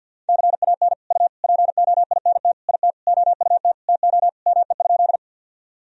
Morse code, heard as a keyed tone, about 51 wpm, 700 Hz. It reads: CAN U POINT IT OUT TO ME?
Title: Sentences from Top 200 Words and Abbreviations 50wpm